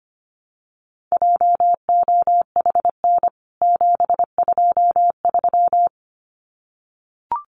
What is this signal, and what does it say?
Morse code, heard as a keyed tone, about 25 wpm, 700 Hz. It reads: JOHN 723
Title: CWT Exchange 25wpm